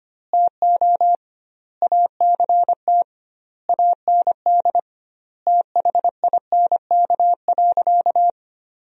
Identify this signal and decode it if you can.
Morse code, heard as a keyed tone, about 25 wpm, 700 Hz. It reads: TO ACT AND THINK.
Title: 2x Word-Spacing - Sentences from Top 500 Words - 25wpm